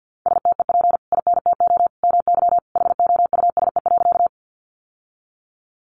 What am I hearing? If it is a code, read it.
Morse code, heard as a keyed tone, about 48 wpm, 700 Hz. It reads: STEP INTO MY HOUSE.
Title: Sentences from Top 600 Words 50wpm